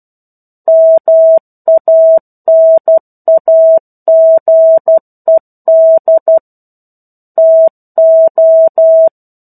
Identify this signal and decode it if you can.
Morse code, heard as a keyed tone, about 12 wpm, 650 Hz. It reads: MANAGED TO
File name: 01-Chapter-01 12 WPM